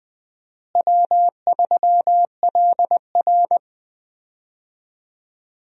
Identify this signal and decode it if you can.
Morse code, heard as a keyed tone, about 20 wpm, 700 Hz. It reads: W3LR
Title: USA Call Signs - 4 Characters - Repeated 3x - 20wpm